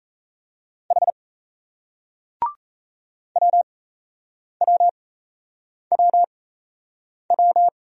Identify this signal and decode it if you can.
Morse code, heard as a keyed tone, about 38 wpm, spaced as for 8 wpm, 700 Hz. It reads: H WWWW